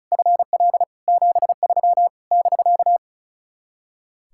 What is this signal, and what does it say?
Morse code, heard as a keyed tone, about 44 wpm, 700 Hz. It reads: FL 73 <BK>
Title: Speed-Racing Common QSO Phrases - 35wpm